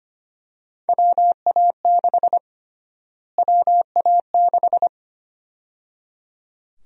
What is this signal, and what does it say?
Morse code, heard as a keyed tone, about 25 wpm, 700 Hz. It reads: WA6 WA6
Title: USA Call Sign Prefixes - Repeated 2x - 25wpm